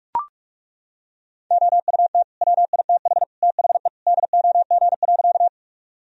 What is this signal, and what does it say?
Morse code, heard as a keyed tone, about 43 wpm, 700 Hz. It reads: OUT WITH THE DOG.
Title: Sentences from Top 1000 Words - Review Part 2 - 45wpm